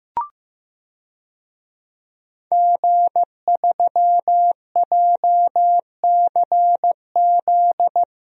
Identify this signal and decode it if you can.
Morse code, heard as a keyed tone, about 15 wpm, 700 Hz. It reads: G3JCZ
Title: DXCC-100 Call Signs - 5 Characters - 15wpm